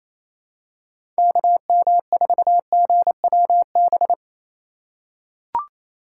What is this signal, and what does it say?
Morse code, heard as a keyed tone, about 28 wpm, 700 Hz. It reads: KM4GWB
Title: USA Call Signs - 6 Characters - 4x Character Spacing - 28wpm